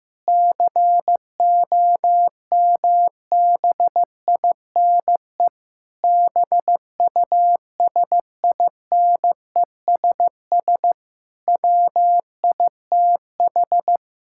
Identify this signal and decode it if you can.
Morse code, heard as a keyed tone, about 16 wpm, 700 Hz. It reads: COMBINE BUSINESS WITH
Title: Sentences from Top 1600 Words 15wpm